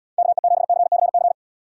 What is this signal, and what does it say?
Morse code, heard as a keyed tone, about 79 wpm, 700 Hz. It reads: KE7CCC